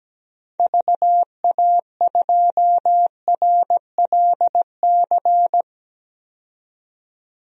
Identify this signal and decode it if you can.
Morse code, heard as a keyed tone, about 17 wpm, 700 Hz. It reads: VA2RLC
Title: DXCC-100 Call Signs - 6 Characters - 17wpm